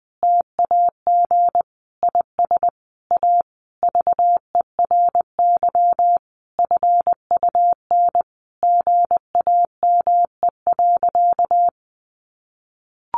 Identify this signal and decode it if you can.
Morse code, heard as a keyed tone, about 20 wpm, 700 Hz. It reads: TAG IS A VERY FUN GAME.
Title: Additional Easy Phrases 20wpm